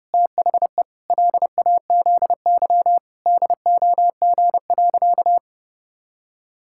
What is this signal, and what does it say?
Morse code, heard as a keyed tone, about 32 wpm, 700 Hz. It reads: THE LAZY DOG.
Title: Pangram Challenge v2 - 30wpm